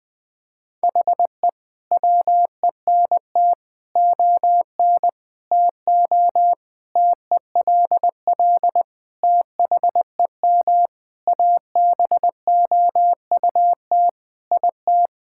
Morse code, heard as a keyed tone, about 20 wpm, 700 Hz. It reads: HE WENT ON TO TELL THEM ABOUT IT
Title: Sentences from Top 1000 Words - Review Part 6 - 20wpm